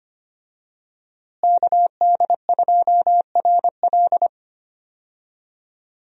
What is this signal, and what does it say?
Morse code, heard as a keyed tone, about 25 wpm, 700 Hz. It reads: KD2RL
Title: USA Call Signs - 5 Characters - Repeated 2x - 25wpm